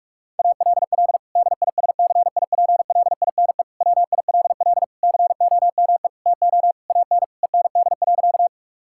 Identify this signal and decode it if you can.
Morse code, heard as a keyed tone, about 46 wpm, 700 Hz. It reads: ALL DISCIPLINE WILL COME TO AN END.
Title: Sentences from Top 1000 Words - Review Part 6 - 45wpm